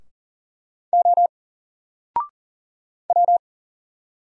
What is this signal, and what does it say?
Morse code, heard as a keyed tone, about 40 wpm, 700 Hz. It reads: O W
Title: Single Letter-Number Alt 40wpm